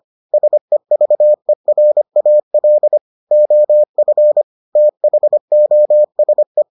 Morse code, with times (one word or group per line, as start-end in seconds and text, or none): 0.3-3.0 SEVERAL
3.3-4.4 OF
4.8-6.6 THOSE